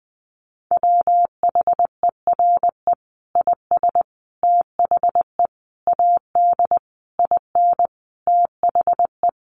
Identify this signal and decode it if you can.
Morse code, heard as a keyed tone, about 20 wpm, 700 Hz. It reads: WHERE IS THE AD IN THE
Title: Sentences from Top 1300 Words 20wpm